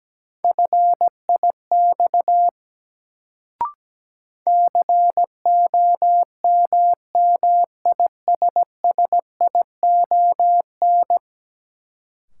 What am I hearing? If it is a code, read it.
Morse code, heard as a keyed tone, about 17 wpm, 700 Hz. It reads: FIX COMMISSION
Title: Sentences from Top 1400 Words - New Word Drill Alt - 17wpm